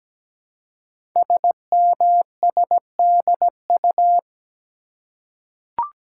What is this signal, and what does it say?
Morse code, heard as a keyed tone, about 17 wpm, 700 Hz. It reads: SMSDU
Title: Code Groups - 5 Letters - 17wpm